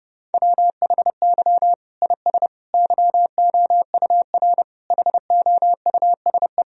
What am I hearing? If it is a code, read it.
Morse code, heard as a keyed tone, about 30 wpm, 700 Hz. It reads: WHY IS YOUR HOUSE
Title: Sentences from Top 1000 Words - Review Part 2 - 30wpm